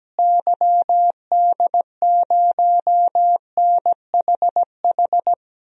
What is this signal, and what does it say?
Morse code, heard as a keyed tone, about 17 wpm, 700 Hz. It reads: YD0NHH
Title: DXCC-100 Call Signs - 6 Characters - Repeated 2x - 17wpm